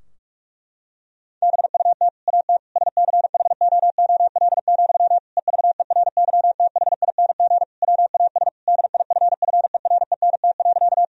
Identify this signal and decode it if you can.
Morse code, heard as a keyed tone, about 46 wpm, 700 Hz. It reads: BUT AT SCHOOL, EVERYTHING WAS DIFFERENT.
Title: Sentences from Top 1000 Words - Review Part 3 - 45wpm